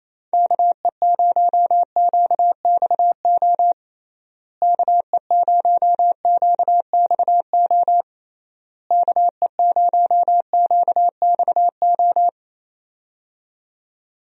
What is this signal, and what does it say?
Morse code, heard as a keyed tone, about 28 wpm, 700 Hz. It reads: KE0QXO KE0QXO KE0QXO
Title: USA Call Signs - 6 Characters - Repeated 3x - 28wpm